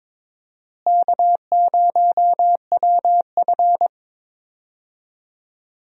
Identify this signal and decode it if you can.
Morse code, heard as a keyed tone, about 22 wpm, 700 Hz. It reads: K0WF
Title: USA Call Signs - 4 Characters - Repeated 3x - 22wpm